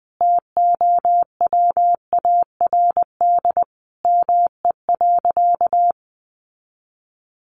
Morse code, heard as a keyed tone, about 20 wpm, 700 Hz. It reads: TOWARD ME.
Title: Sentences from Top 1300 Words 20wpm